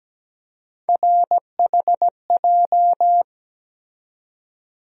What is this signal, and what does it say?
Morse code, heard as a keyed tone, about 17 wpm, 700 Hz. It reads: RHJ
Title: USA Call Sign Postfixes - 3 Letters 17wpm